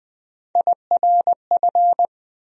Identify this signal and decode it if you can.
Morse code, heard as a keyed tone, about 20 wpm, 700 Hz. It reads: IRF
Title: USA Call Sign Postfixes - 3 Letters 20wpm